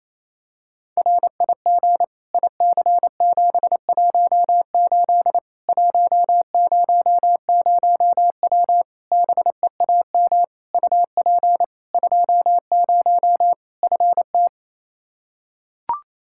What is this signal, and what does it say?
Morse code, heard as a keyed tone, about 27 wpm, 700 Hz. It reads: RIG IC718 100W BEAM UP 20 FT
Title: K1USN SST QSO Practice 28wpm